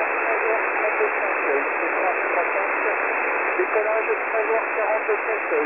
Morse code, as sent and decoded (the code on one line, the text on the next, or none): none